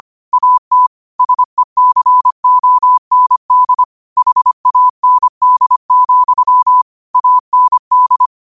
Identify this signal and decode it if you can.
Morse code, heard as a keyed tone, about 25 wpm, 1000 Hz. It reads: AT SECOND HAND, AND
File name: qotdfiles / 25wpm